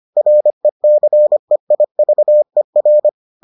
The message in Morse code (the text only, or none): RECEIVER